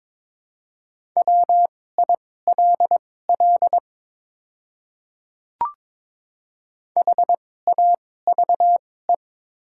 Morse code, heard as a keyed tone, about 22 wpm, spaced as for 16 wpm, 690 Hz. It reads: WILL HAVE
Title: 2x Character Spacing - Top 100 Words - 22wpm